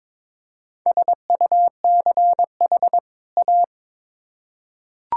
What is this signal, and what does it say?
Morse code, heard as a keyed tone, about 22 wpm, 700 Hz. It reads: SUCH A
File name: Build A Phrase - 2-Word Phrases - 22wpm